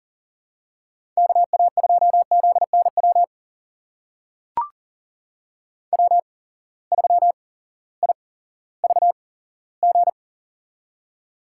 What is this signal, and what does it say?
Morse code, heard as a keyed tone, about 40 wpm, spaced as for 18 wpm, 700 Hz. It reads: KA2ZNW W3IVG
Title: USA Call Signs - 8x Character Spacing - 40wpm